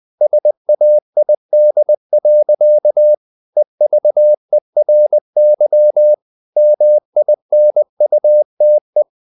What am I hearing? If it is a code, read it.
Morse code, heard as a keyed tone, about 20 wpm, 600 Hz. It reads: SAID. EVERY MINUTE
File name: the old man and the sea part 2 20wpm